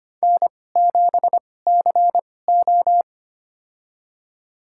Morse code, heard as a keyed tone, about 25 wpm, spaced as for 18 wpm, 700 Hz. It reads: N7CO
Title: USA Call Signs - 4 Characters - 2x Character Spacing - 25wpm